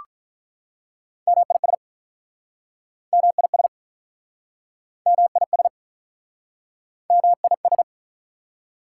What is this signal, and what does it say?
Morse code, heard as a keyed tone, about 46 wpm, 700 Hz. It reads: MIS MIS MIS MIS